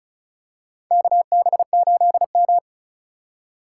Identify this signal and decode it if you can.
Morse code, heard as a keyed tone, about 34 wpm, 700 Hz. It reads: KB8M
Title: K1USN SST QSO Practice - 35wpm